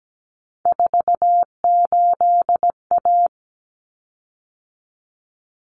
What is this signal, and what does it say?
Morse code, heard as a keyed tone, about 17 wpm, 700 Hz. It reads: 48A
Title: Common Cut Numbers - 3-Digit 17wpm